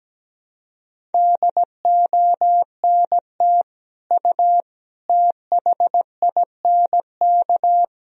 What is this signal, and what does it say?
Morse code, heard as a keyed tone, about 17 wpm, 700 Hz. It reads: DONT U THINK
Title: Sentences from Top 300 Words and Abbreviations 17wpm